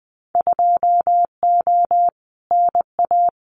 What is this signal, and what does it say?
Morse code, heard as a keyed tone, about 19 wpm, 700 Hz. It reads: 2O NA
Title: Winter Field Day Exchanges 20wpm